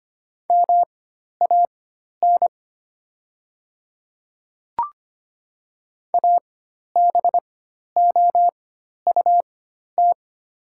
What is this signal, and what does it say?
Morse code, heard as a keyed tone, about 25 wpm, spaced as for 12 wpm, 690 Hz. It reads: MAN ABOUT
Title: 4x Character Spacing - Top 500 Words - 25wpm